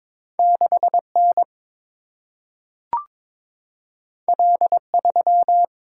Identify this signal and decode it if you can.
Morse code, heard as a keyed tone, about 22 wpm, 700 Hz. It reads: 6N L3
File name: practice / 22wpm / Two Letter-Number Combinations - 22wpm